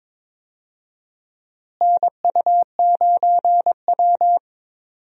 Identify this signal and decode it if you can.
Morse code, heard as a keyed tone, about 21 wpm, 700 Hz. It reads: NU9W
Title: USA Call Signs - 4 Characters - Repeated 3x - 22wpm